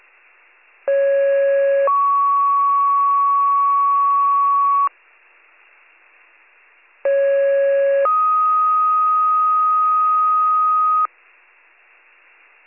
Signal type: two-tone page

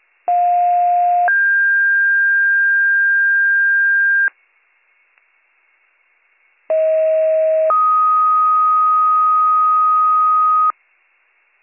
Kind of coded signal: two-tone page

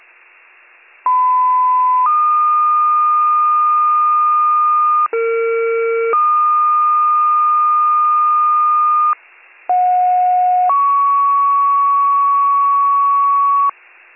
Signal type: two-tone page

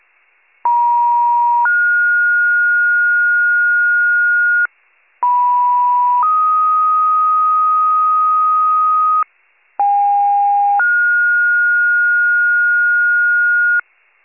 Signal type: two-tone page

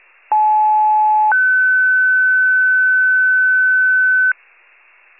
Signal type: two-tone page

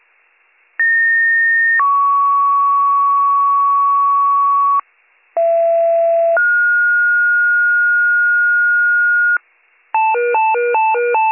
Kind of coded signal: two-tone page

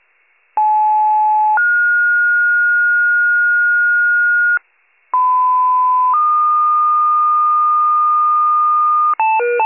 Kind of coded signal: two-tone page

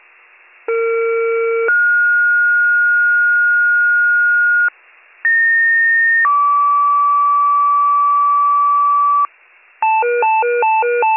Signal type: two-tone page

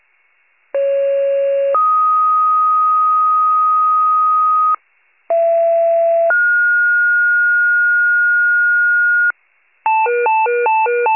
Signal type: two-tone page

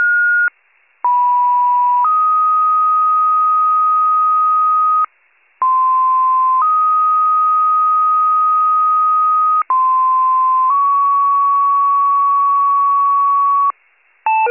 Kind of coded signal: two-tone page

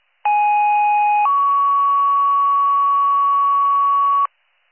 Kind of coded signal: two-tone page

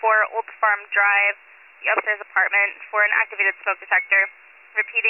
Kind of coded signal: Emergency Alert System (SAME) tones

